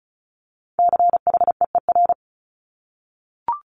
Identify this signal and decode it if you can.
Morse code, heard as a keyed tone, about 36 wpm, 700 Hz. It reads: CHEER
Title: Multi-letter Phonemes Repeated 3x 35wpm